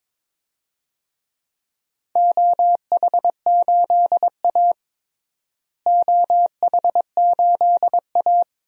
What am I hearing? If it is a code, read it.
Morse code, heard as a keyed tone, about 22 wpm, 700 Hz. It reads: OH8A OH8A
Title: DXCC-100 Call Signs - 4 Characters - Repeated 2x - 22wpm